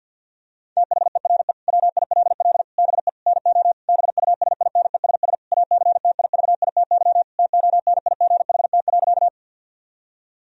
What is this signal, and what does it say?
Morse code, heard as a keyed tone, about 52 wpm, 700 Hz. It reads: THERE WILL BE NO BUSINESS ACTIVITY TONIGHT.
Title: Sentences from Top 1000 Words - Review Part 4 - 50wpm